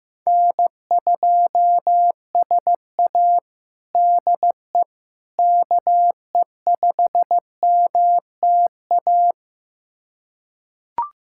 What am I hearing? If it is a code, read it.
Morse code, heard as a keyed tone, about 16 wpm, 700 Hz. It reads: N2SA DE KE5MTA
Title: USA Callsign DE Callsign 15wpm